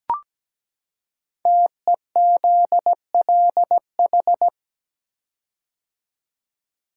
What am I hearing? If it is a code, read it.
Morse code, heard as a keyed tone, about 17 wpm, 700 Hz. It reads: TEZLH
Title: Code Groups - 5 Letters - 17wpm